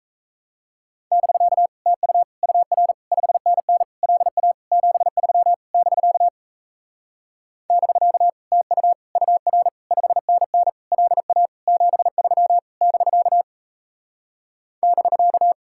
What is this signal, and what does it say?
Morse code, heard as a keyed tone, about 38 wpm, 700 Hz. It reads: <BK> TU UR 5NN LA 73 <BK> <BK> TU UR 5NN LA 73 <BK> <BK>